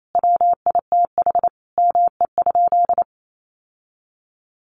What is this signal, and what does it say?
Morse code, heard as a keyed tone, about 28 wpm, 700 Hz. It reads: WITH ME?
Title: Sentences from Top 1500 Words 28wpm